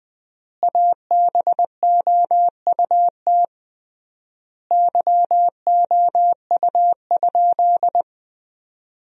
Morse code, heard as a keyed tone, about 21 wpm, 700 Hz. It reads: ABOUT YOU?
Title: 3x Word-Spacing - Sentences from Top 100 Words - 20wpm